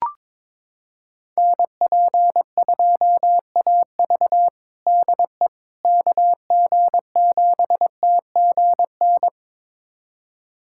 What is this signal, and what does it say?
Morse code, heard as a keyed tone, about 22 wpm, 700 Hz. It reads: NP2AV DE KG7TGN
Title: USA Callsign DE Callsign 22wpm